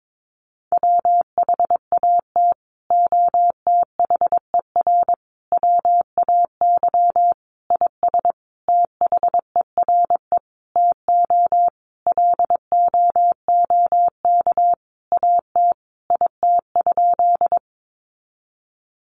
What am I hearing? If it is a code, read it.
Morse code, heard as a keyed tone, about 22 wpm, 700 Hz. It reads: WHAT OTHER WAY IS THERE TO LOOK AT IT?
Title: Sentences from Top 100 Words 22wpm